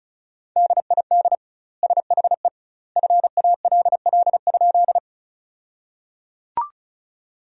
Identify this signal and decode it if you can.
Morse code, heard as a keyed tone, about 34 wpm, 700 Hz. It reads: DID SHE FALL?
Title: 2x Word-Spacing - Sentences from Top 500 Words - 35wpm